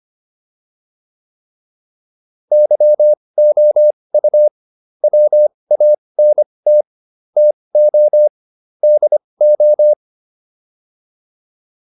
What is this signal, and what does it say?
Morse code, heard as a keyed tone, about 25 wpm, spaced as for 20 wpm, 600 Hz. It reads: YOU WANT TO DO